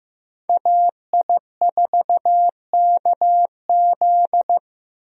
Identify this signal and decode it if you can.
Morse code, heard as a keyed tone, about 15 wpm, 700 Hz. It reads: AI4KZ
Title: USA Call Signs - 5 Characters - Repeated 3x - 15wpm